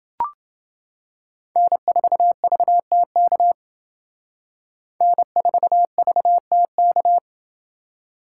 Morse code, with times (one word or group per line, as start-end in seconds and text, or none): 1.6-3.5 N4VTK
5.0-7.2 N4VTK